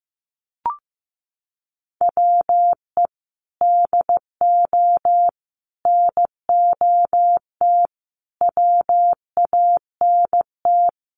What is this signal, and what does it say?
Morse code, heard as a keyed tone, about 15 wpm, 700 Hz. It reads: WE DO NOT WANT
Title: Sets of 5 Words 15wpm